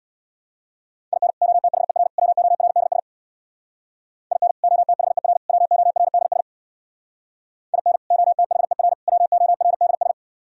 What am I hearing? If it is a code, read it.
Morse code, heard as a keyed tone, about 71 wpm, 700 Hz. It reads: IN OTHER WORDS IN OTHER WORDS IN OTHER WORDS